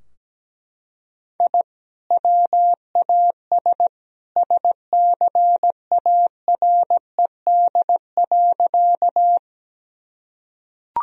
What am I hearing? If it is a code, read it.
Morse code, heard as a keyed tone, about 17 wpm, 700 Hz. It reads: I WAS SCARED.
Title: Sentences from Top 1000 Words - Review Part 4 - 17wpm